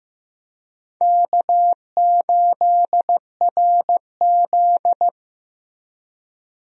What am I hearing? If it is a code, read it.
Morse code, heard as a keyed tone, about 16 wpm, 700 Hz. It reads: K8RZ
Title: USA Call Signs - 4 Characters - Repeated 3x - 15wpm